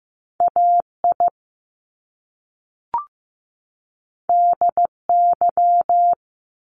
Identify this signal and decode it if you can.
Morse code, heard as a keyed tone, about 15 wpm, 700 Hz. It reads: AI DY
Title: Top Two Letter Combinations in Words 15wpm